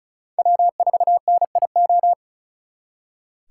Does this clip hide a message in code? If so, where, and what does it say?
Yes: Morse code from 0.4 to 2.1 s, heard as a keyed tone, about 45 wpm, 700 Hz. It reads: W4NIO